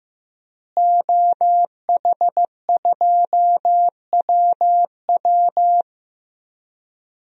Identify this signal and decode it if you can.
Morse code, heard as a keyed tone, about 16 wpm, 700 Hz. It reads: OH2WW
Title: DXCC-100 Call Signs - 5 Characters - Repeated 3x - 15wpm